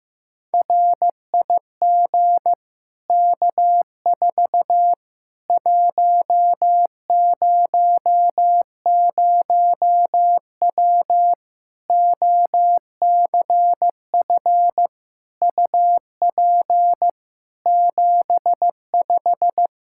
Morse code, heard as a keyed tone, about 15 wpm, 700 Hz. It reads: RIG K4 100W OCF UP 75
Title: Practice QSO - Exchange 2 - Info - 15wpm